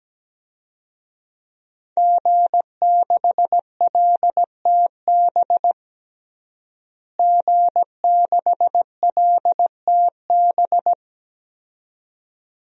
Morse code, heard as a keyed tone, about 17 wpm, 700 Hz. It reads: G6LTB G6LTB
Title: DXCC-100 Call Signs - 5 Characters - Repeated 2x - 17wpm